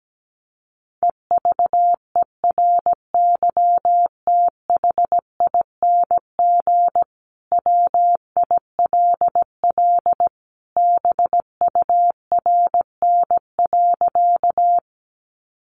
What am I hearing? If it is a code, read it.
Morse code, heard as a keyed tone, about 17 wpm, 700 Hz. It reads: EVERYTHING WILL BURN.
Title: Sentences from Top 1300 Words 17wpm